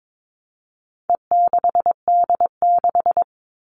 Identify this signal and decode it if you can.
Morse code, heard as a keyed tone, about 22 wpm, 700 Hz. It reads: E6D6
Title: All Cut Numbers - 4-Digit 22wpm